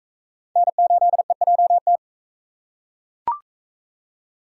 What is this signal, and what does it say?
Morse code, heard as a keyed tone, about 37 wpm, 700 Hz. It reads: N8EJT